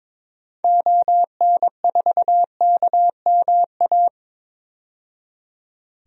Morse code, heard as a keyed tone, about 21 wpm, 700 Hz. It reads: ON4KMA
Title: DXCC-100 Call Signs - 6 Characters - 22wpm